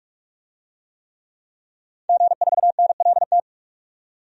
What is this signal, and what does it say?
Morse code, heard as a keyed tone, about 46 wpm, 700 Hz. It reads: G4NLT
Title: DXCC-100 Call Signs - 5 Characters - 45wpm